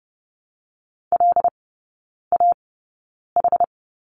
Morse code, heard as a keyed tone, about 30 wpm, spaced as for 9 wpm, 690 Hz. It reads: LAH
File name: Lesson 17e - Letter u - Three Characters 30wpm